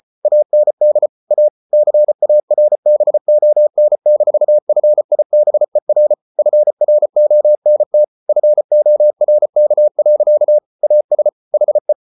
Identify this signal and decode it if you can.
Morse code, heard as a keyed tone, about 34 wpm, 600 Hz. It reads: AND A CARBON-FIBER FRONT FORK. AS HE